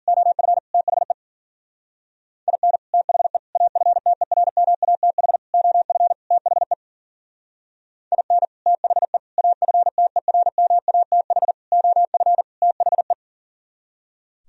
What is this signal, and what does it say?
Morse code, heard as a keyed tone, about 52 wpm, 700 Hz. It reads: OF THE IN THE AFTERMATH OF THE IN THE AFTERMATH OF THE